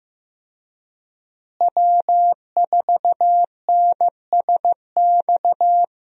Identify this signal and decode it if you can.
Morse code, heard as a keyed tone, about 15 wpm, 700 Hz. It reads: W4NSX
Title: USA Call Signs - 5 Characters - Repeated 3x - 15wpm